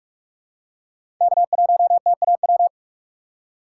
Morse code, heard as a keyed tone, about 43 wpm, 700 Hz. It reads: K1TAW
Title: USA Call Signs - 5 Characters - 3x Character Spacing - 45wpm